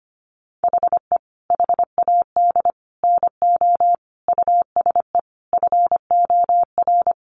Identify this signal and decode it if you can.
Morse code, heard as a keyed tone, about 25 wpm, 700 Hz. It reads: HE HAD NO USE FOR
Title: Sentences from Top 1200 Words 25wpm